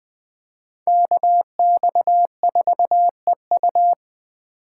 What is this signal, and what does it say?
Morse code, heard as a keyed tone, about 24 wpm, 700 Hz. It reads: KX4EU
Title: Speed Racing - USA 2x2 Call Signs 20wpm